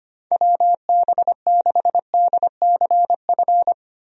Morse code, heard as a keyed tone, about 25 wpm, 700 Hz. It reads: WB6DCF
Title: USA Call Signs - 6 Characters - 25wpm